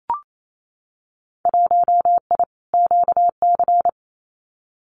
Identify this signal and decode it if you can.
Morse code, heard as a keyed tone, about 28 wpm, 700 Hz. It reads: 1I QC
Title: Winter Field Day Exchanges 28wpm